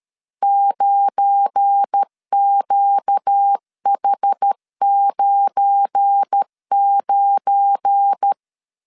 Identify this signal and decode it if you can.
Morse code, heard as a keyed tone, about 13 wpm, 800 Hz. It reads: 9QH99
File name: LEZIONI / 7 GF9Z / 7 GF9 12wpm 2319